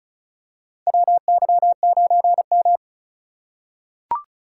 Morse code, heard as a keyed tone, about 34 wpm, 700 Hz. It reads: WY9M
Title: USA Call Signs - 4 Characters - 2x Character Spacing - 35wpm